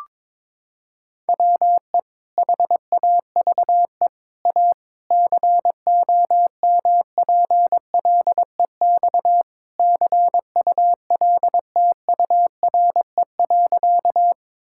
Morse code, heard as a keyed tone, about 21 wpm, 700 Hz. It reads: WE HAVE A COMPLEX CULTURE.
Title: Sentences from Top 1000 Words - Review Part 6 - 22wpm